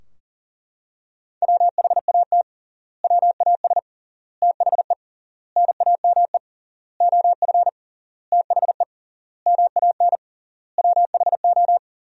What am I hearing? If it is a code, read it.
Morse code, heard as a keyed tone, about 40 wpm, 700 Hz. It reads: WHAT WAS THE NAME OF THE MAN WHO
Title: 3x Word-Spacing - Sentences from Top 300 Words - 40wpm